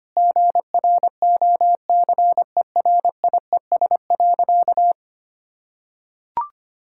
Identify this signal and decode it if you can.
Morse code, heard as a keyed tone, about 25 wpm, 690 Hz. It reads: GROCERIES.
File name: Easy Sentences with Hints - 25wpm